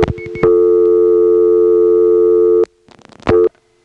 Dial tone (North America), 350 + 440 Hz.